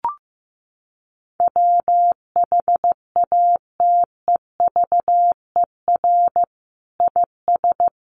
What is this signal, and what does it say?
Morse code, heard as a keyed tone, about 15 wpm, 700 Hz. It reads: WHATEVER IS
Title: Sentences from Top 1100 Words 15wpm